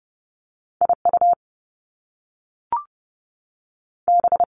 Morse code, heard as a keyed tone, about 30 wpm, 700 Hz. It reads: I U B
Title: Top Two Letter Combinations in Words 30wpm